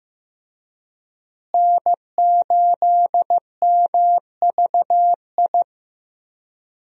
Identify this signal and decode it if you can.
Morse code, heard as a keyed tone, about 15 wpm, 700 Hz. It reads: N8MVI
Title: USA Call Signs - 5 Characters - Repeated 2x - 15wpm